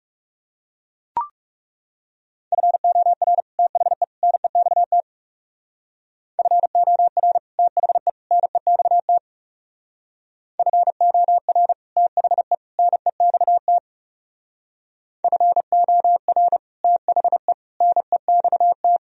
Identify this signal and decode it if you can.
Morse code, heard as a keyed tone, about 37 wpm, 700 Hz. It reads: FOR THE NEXT FOR THE NEXT FOR THE NEXT FOR THE NEXT
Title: Speed-Racing - Sets of 3 Words 30wpm